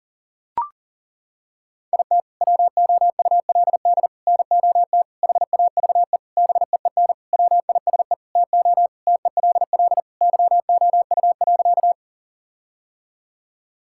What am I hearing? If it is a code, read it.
Morse code, heard as a keyed tone, about 40 wpm, 700 Hz. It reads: IT WOULD NOT HAVE BEEN WISE TO TELL YOU.
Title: Sentences from Top 1600 Words 40wpm